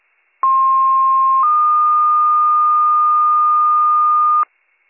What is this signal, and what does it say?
Two-tone page, heard as a one-second tone, then a three-second tone, about 1060 then 1220 Hz, with noise. It is heard from 0.4 to 4.4 s.